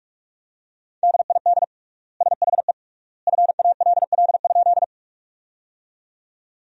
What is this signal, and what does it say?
Morse code, heard as a keyed tone, about 47 wpm, 700 Hz. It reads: DID SHE FALL?
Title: 3x Word-Spacing - Sentences from Top 500 Words - 45wpm